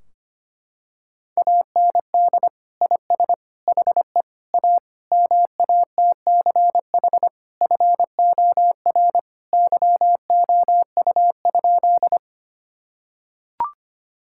Morse code, heard as a keyed tone, about 25 wpm, 700 Hz. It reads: AND IS HE A MATCH FOR YOU?
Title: Sentences from Top 1600 Words 25wpm